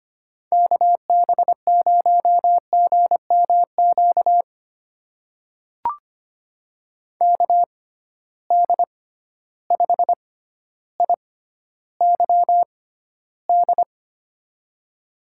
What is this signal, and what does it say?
Morse code, heard as a keyed tone, about 25 wpm, spaced as for 13 wpm, 700 Hz. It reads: KB0GMQ KD5IYD
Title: USA Call Signs - 6 Characters - 6x Character Spacing - 25wpm